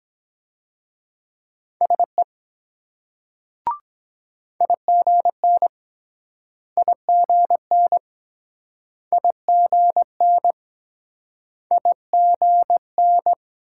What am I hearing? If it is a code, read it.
Morse code, heard as a keyed tone, about 22 wpm, 690 Hz. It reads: SE IGN IGN IGN IGN